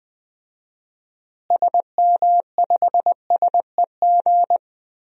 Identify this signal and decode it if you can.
Morse code, heard as a keyed tone, about 20 wpm, 700 Hz. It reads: SM5SEG
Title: DXCC-100 Call Signs - 6 Characters - Repeated 3x - 20wpm